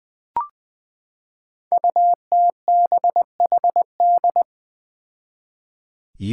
Morse code, heard as a keyed tone, about 20 wpm, 700 Hz. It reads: UTBHD
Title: Five Challenging Letter-Number Combinations 20wpm